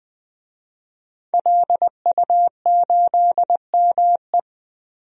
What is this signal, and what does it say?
Morse code, heard as a keyed tone, about 20 wpm, 700 Hz. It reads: LU8ME
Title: DXCC-100 Call Signs - 5 Characters - Repeated 2x - 20wpm